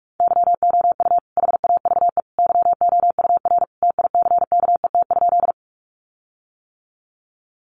Morse code, heard as a keyed tone, about 44 wpm, 700 Hz. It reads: YOU HAVE YOUR TICKET?
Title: Sentences from Top 1400 Words 45wpm